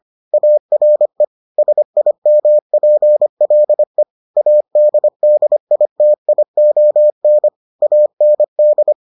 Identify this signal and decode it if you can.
Morse code, heard as a keyed tone, about 25 wpm, 600 Hz. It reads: ARE SIMPLE ADDITION AND